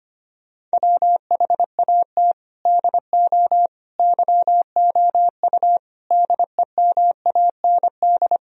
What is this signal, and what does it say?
Morse code, heard as a keyed tone, about 25 wpm, 700 Hz. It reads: WHAT DO YOU DEMAND
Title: Sentences from Top 1000 Words - Review Part 4 - 25wpm